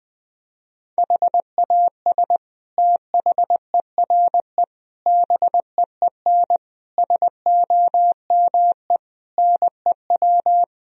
Morse code, heard as a keyed tone, about 20 wpm, 700 Hz. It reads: HAS THERE BEEN SOME NEW
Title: Sentences from Top 1000 Words - Review Part 4 - 20wpm